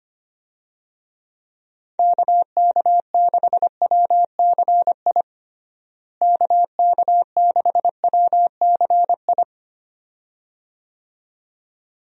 Morse code, heard as a keyed tone, about 25 wpm, 700 Hz. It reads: KK6WCI KK6WCI
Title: USA Call Signs - 6 Characters - Repeated 2x - 25wpm